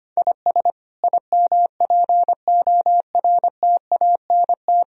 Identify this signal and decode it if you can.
Morse code, heard as a keyed tone, about 26 wpm, 700 Hz. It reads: IS IMPORTANT
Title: Speed-Racing - Sets of 5 Words 25wpm